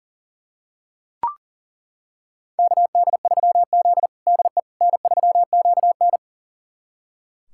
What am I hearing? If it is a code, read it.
Morse code, heard as a keyed tone, about 40 wpm, 700 Hz. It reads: KD3Z DE N3QN
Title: USA Callsign DE Callsign - 4 Characters - 40wpm